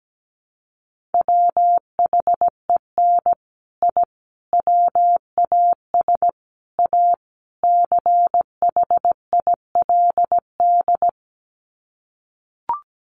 Morse code, heard as a keyed tone, about 17 wpm, 700 Hz. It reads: WHEN I WAS A CHILD
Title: Sets of 5 Words Encore 17wpm